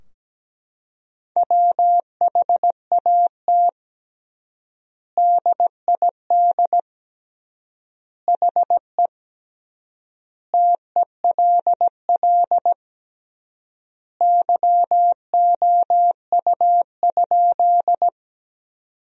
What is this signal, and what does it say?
Morse code, heard as a keyed tone, about 17 wpm, 700 Hz. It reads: WHAT DID HE TELL YOU?
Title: 3x Word-Spacing - Sentences from Top 200 Words - 17wpm